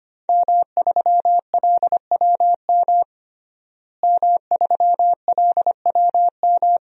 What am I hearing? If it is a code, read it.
Morse code, heard as a keyed tone, about 25 wpm, 700 Hz. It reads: M3LWM M3LWM
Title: DXCC-100 Call Signs - 5 Characters - Repeated 2x - 25wpm